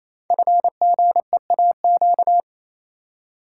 Morse code, heard as a keyed tone, about 27 wpm, 700 Hz. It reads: FGEAQ
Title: Code Groups - 5 Letters - 28wpm